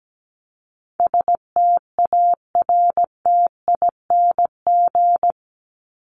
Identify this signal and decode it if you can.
Morse code, heard as a keyed tone, about 17 wpm, 700 Hz. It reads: STARTING
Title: ING Suffix - Top 100 Words with Base Words 17wpm